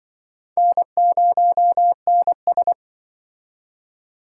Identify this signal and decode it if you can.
Morse code, heard as a keyed tone, about 25 wpm, 700 Hz. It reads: N0NS